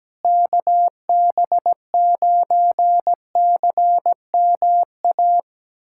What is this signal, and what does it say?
Morse code, heard as a keyed tone, about 17 wpm, 700 Hz. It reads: KB9CMA
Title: USA Callsign DE Callsign - 6 Characters - 17wpm